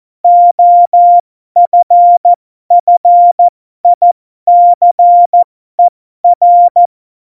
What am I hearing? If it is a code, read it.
Morse code, heard as a keyed tone, about 14 wpm, spaced as for 12 wpm, 700 Hz. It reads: OFFICER